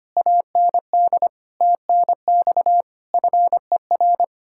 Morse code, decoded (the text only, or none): AND TNX FER